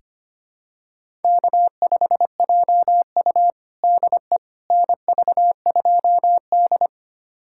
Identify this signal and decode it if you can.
Morse code, heard as a keyed tone, about 25 wpm, 700 Hz. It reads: K5JU DE NV2D
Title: USA Callsign DE Callsign - 4 Characters - 25wpm